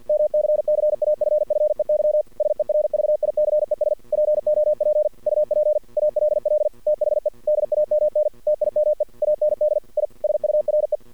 Morse code, heard as a keyed tone, about 49 wpm, 600 Hz. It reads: MCCAFFEY SERVICES POP UP ALL THE DAMN TIME AND I SURE
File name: archived / CRSnet